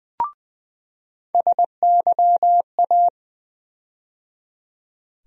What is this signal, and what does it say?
Morse code, heard as a keyed tone, about 20 wpm, 700 Hz. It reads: SYA
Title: USA Call Sign Postfixes - 3 Letters 20wpm